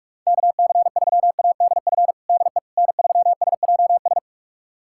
Morse code, heard as a keyed tone, about 46 wpm, 700 Hz. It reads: KC3ADF DE N3SJS
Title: USA Callsign DE Callsign 45wpm